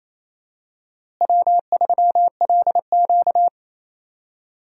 Morse code, heard as a keyed tone, about 28 wpm, 700 Hz. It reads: W3LQ
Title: USA Call Signs - 4 Characters - Repeated 2x - 28wpm